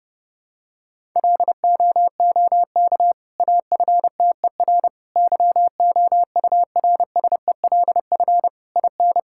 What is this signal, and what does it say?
Morse code, heard as a keyed tone, about 32 wpm, 700 Hz. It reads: LOOK AFTER YOURSELF IN
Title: Sentences from Top 1000 Words - Review Part 6 - 30wpm